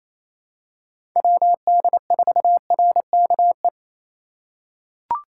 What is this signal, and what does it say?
Morse code, heard as a keyed tone, about 28 wpm, 700 Hz. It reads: WD4RKE
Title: USA Call Signs - 6 Characters - 3x Character Spacing - 28wpm